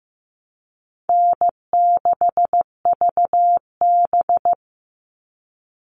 Morse code, heard as a keyed tone, about 15 wpm, 700 Hz. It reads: N6VB
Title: All Cut Numbers - 4-Digit 15wpm